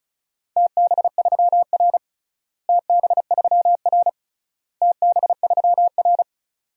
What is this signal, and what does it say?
Morse code, heard as a keyed tone, about 36 wpm, 700 Hz. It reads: TB3R TB3R TB3R
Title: DXCC-100 Call Signs - 4 Characters - Repeated 3x - 35wpm